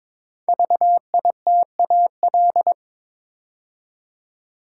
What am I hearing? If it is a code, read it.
Morse code, heard as a keyed tone, about 23 wpm, 700 Hz. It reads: VITAL
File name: 5 Letter Words - First 3 Letters Given - 22wpm